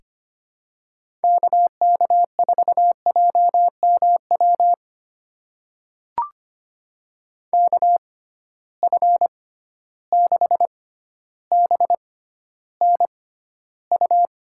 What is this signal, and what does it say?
Morse code, heard as a keyed tone, about 25 wpm, spaced as for 13 wpm, 700 Hz. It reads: KK4JMW KF6BNU
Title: USA Call Signs - 6 Characters - 6x Character Spacing - 25wpm